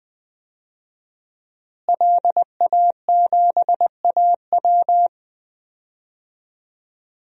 Morse code, heard as a keyed tone, about 20 wpm, 700 Hz. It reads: LA7AW
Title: DXCC-100 Call Signs - 5 Characters - Repeated 3x - 20wpm